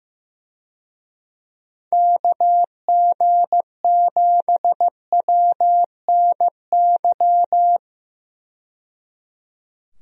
Morse code, heard as a keyed tone, about 15 wpm, 700 Hz. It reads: KG7WNY
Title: USA Call Signs - 6 Characters - 15wpm